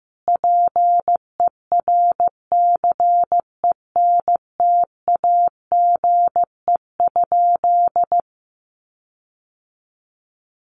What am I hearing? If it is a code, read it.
Morse code, heard as a keyed tone, about 15 wpm, 700 Hz. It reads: PERCENTAGE?
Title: Sentences from Top 1300 Words 15wpm